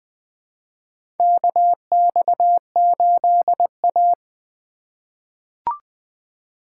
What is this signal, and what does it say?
Morse code, heard as a keyed tone, about 20 wpm, 700 Hz. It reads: KX8A
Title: USA Call Signs - 4 Characters - 3x Character Spacing - 20wpm